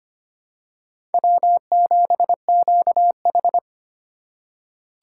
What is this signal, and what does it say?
Morse code, heard as a keyed tone, about 25 wpm, 700 Hz. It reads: W7QH
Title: USA Call Signs - 3x Character Spacing - 25wpm